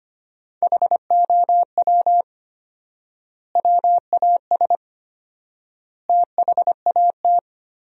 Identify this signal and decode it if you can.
Morse code, heard as a keyed tone, about 25 wpm, 700 Hz. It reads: HOW WAS THAT
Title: 4x Word-Spacing - Sentences from Top 500 Words - 25wpm